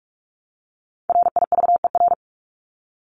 Morse code, heard as a keyed tone, about 44 wpm, 700 Hz. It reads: RIVER
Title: Top 500 Words 45wpm